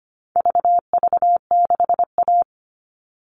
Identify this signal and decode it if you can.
Morse code, heard as a keyed tone, about 25 wpm, 700 Hz. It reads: VV6A
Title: All Cut Numbers - 4-Digit 25wpm